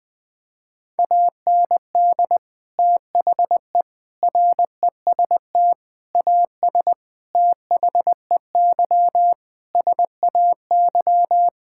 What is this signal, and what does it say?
Morse code, heard as a keyed tone, about 21 wpm, 700 Hz. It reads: AND THE REST AS THEY SAY
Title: Sets of 8 Words - 20wpm